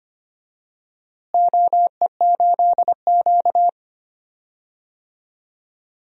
Morse code, heard as a keyed tone, about 25 wpm, 700 Hz. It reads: OE8Q
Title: DXCC-100 Call Signs - 4 Characters - Repeated 2x - 25wpm